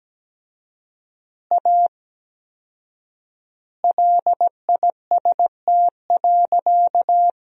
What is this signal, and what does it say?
Morse code, heard as a keyed tone, about 17 wpm, 700 Hz. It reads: A LIST.
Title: 4x Word-Spacing - Sentences from Top 400 Words - 17wpm